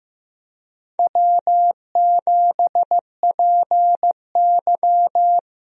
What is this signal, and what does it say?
Morse code, heard as a keyed tone, about 15 wpm, 700 Hz. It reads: W7PY
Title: USA Call Signs - 4 Characters - 15wpm